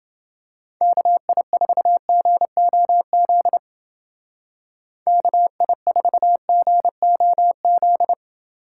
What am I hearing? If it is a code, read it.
Morse code, heard as a keyed tone, about 29 wpm, 700 Hz. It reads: KI4GOZ KI4GOZ